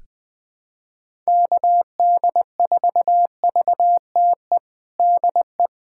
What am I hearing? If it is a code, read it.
Morse code, heard as a keyed tone, about 20 wpm, 700 Hz. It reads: KD4VTE DE
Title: USA Callsign DE Callsign - 6 Characters - 20wpm